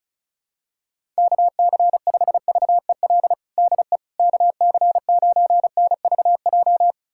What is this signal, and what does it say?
Morse code, heard as a keyed tone, about 34 wpm, 700 Hz. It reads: KC5VEL DE KC9NVJ
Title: USA Callsign DE Callsign - 6 Characters - 35wpm